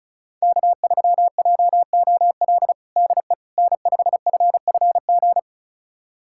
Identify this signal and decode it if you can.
Morse code, heard as a keyed tone, about 36 wpm, 700 Hz. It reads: K3JOL DE N5FFG
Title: USA Callsign DE Callsign - 5 Characters - 35wpm